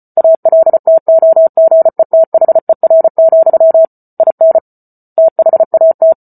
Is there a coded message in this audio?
Yes: Morse code from 0.2 to 6.1 s, heard as a keyed tone, about 34 wpm, 650 Hz. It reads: ALTOGETHER, IN THAT